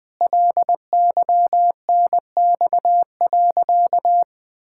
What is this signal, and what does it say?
Morse code, heard as a keyed tone, about 20 wpm, 700 Hz. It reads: LYNX.
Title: Pangram Challenge v2 - 20wpm